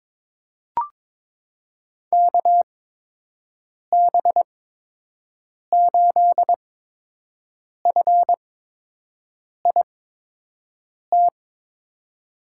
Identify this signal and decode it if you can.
Morse code, heard as a keyed tone, about 22 wpm, spaced as for 6 wpm, 700 Hz. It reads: KB8FIT
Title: USA Call Signs - 8x Character Spacing - 22wpm